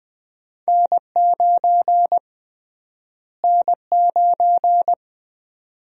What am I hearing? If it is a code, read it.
Morse code, heard as a keyed tone, about 20 wpm, 700 Hz. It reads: N9 N9
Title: USA Call Sign Prefixes - Repeated 2x - 20wpm